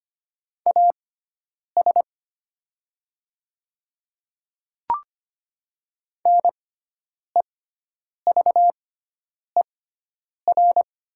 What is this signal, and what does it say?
Morse code, heard as a keyed tone, about 25 wpm, spaced as for 9 wpm, 700 Hz. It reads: AS NEVER
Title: 6x Character Spacing - Top 300 Words - 25wpm